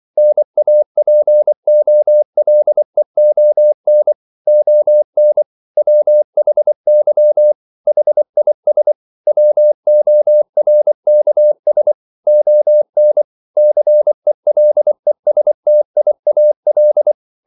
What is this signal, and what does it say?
Morse code, heard as a keyed tone, about 24 wpm, 600 Hz. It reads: NAPOLEON ON WHY HIS WORKS ON CELESTIAL
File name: CW / Practice / EnglishQuotations0424